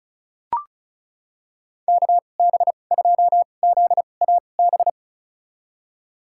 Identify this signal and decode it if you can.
Morse code, heard as a keyed tone, about 36 wpm, spaced as for 25 wpm, 700 Hz. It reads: KB2ZAB
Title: USA Call Signs - 6 Characters - 2x Character Spacing - 35wpm